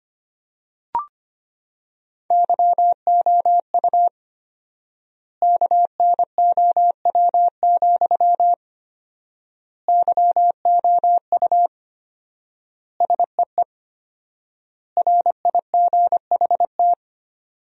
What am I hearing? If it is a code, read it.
Morse code, heard as a keyed tone, about 25 wpm, 700 Hz. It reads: YOU KNOW, YOU SEE RIGHT
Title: 4x Word-Spacing - Sentences from Top 200 Words - 25wpm